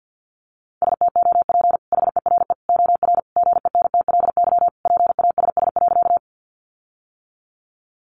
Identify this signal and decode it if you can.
Morse code, heard as a keyed tone, about 50 wpm, 700 Hz. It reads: STOP HERE OR GENTLY PASS.
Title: Sentences from Top 1500 Words 50wpm